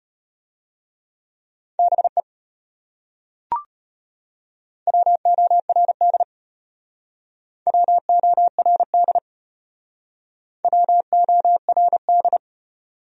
Morse code, heard as a keyed tone, about 34 wpm, 700 Hz. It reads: BE WORD WORD WORD